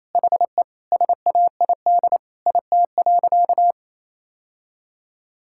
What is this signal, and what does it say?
Morse code, heard as a keyed tone, about 28 wpm, 700 Hz. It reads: HE SAID IT.
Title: Sentences from Top 1600 Words 28wpm